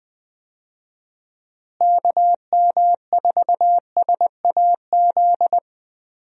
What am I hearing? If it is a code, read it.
Morse code, heard as a keyed tone, about 20 wpm, 700 Hz. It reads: KM4SAZ
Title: USA Call Signs - 6 Characters - Repeated 2x - 20wpm